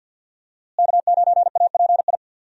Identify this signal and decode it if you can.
Morse code, heard as a keyed tone, about 46 wpm, 700 Hz. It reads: K9API
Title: USA Call Signs - 5 Characters - Repeated 3x - 50wpm